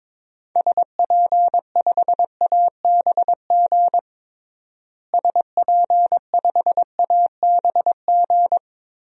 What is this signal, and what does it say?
Morse code, heard as a keyed tone, about 22 wpm, 700 Hz. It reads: SP5ABG SP5ABG
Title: DXCC-100 Call Signs - 6 Characters - Repeated 2x - 22wpm